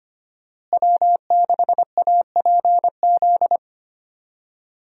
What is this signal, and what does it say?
Morse code, heard as a keyed tone, about 25 wpm, 700 Hz. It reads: W6APZ
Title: USA Call Signs - 5 Characters - Repeated 3x - 25wpm